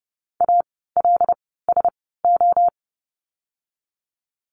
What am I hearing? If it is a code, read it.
Morse code, heard as a keyed tone, about 32 wpm, spaced as for 17 wpm, 700 Hz. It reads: ALSO
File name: course / 30:10wpm / 108 Lesson 18f - Letter m - Words 30wpm